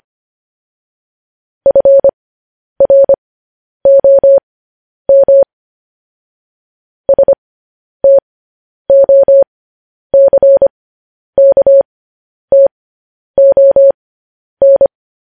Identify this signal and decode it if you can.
Morse code, heard as a keyed tone, about 25 wpm, spaced as for 10 wpm, 550 Hz. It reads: FROM STOCKTON
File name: podcast / News headlines-1x-25x10